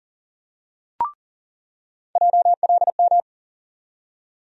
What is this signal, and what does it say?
Morse code, heard as a keyed tone, about 40 wpm, 700 Hz. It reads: JLM
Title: USA Call Sign Postfixes - 3 Letters 40wpm